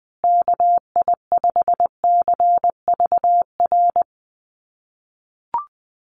Morse code, heard as a keyed tone, about 20 wpm, 700 Hz. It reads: KI5CVR
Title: USA Call Signs Repeated 2x 20wpm